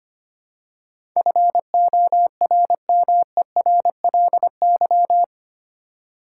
Morse code, heard as a keyed tone, about 25 wpm, 690 Hz. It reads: FORMERLY
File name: 8 Letter Words - First 3 Letters Given - 25wpm